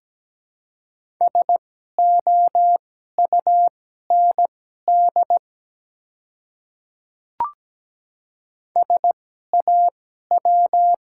Morse code, heard as a keyed tone, about 17 wpm, spaced as for 12 wpm, 700 Hz. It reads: SOUND SAW
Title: 2x Character Spacing - Top 300 Words - 17wpm